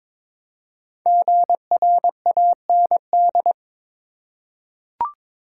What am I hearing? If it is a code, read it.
Morse code, heard as a keyed tone, about 22 wpm, 700 Hz. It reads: GRAND
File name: 5 Letter Words - First 3 Letters Given - 22wpm